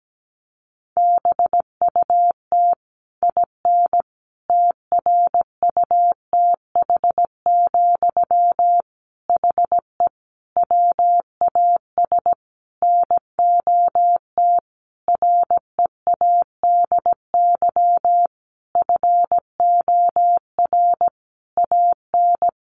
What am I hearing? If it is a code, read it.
Morse code, heard as a keyed tone, about 17 wpm, 700 Hz. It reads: BUT IN TRUTH, HE WAS NOT READY FOR AN
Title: Sentences from Top 1500 Words 17wpm